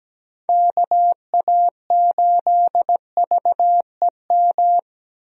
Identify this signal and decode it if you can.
Morse code, heard as a keyed tone, about 17 wpm, 700 Hz. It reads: KA8VEM
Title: USA Call Signs - 6 Characters - Repeated 2x - 17wpm